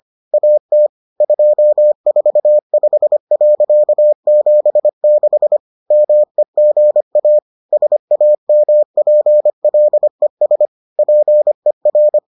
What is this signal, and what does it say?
Morse code, heard as a keyed tone, about 25 wpm, 600 Hz. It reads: AT 245.76 MEGA SAMPLES PER